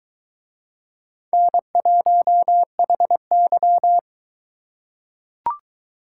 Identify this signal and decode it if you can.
Morse code, heard as a keyed tone, about 22 wpm, 700 Hz. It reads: N1HY